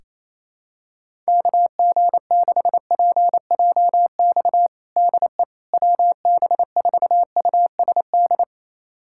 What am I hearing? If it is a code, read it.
Morse code, heard as a keyed tone, about 28 wpm, 700 Hz. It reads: KG6PJX DE WB4USD
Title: USA Callsign DE Callsign - 6 Characters - 28wpm